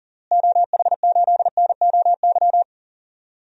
Morse code, heard as a keyed tone, about 40 wpm, 700 Hz. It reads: OH8NOY
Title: DXCC-100 Call Signs - 6 Characters - Repeated 3x - 40wpm